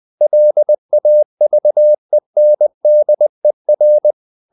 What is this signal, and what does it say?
Morse code, heard as a keyed tone, about 20 wpm, 600 Hz. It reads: LAVENDER
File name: the old man and the sea / the old man and the sea part 1 20wpm